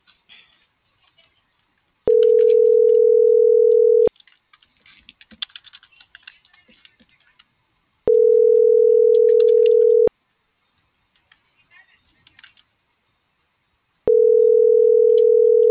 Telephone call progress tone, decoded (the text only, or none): ringback tone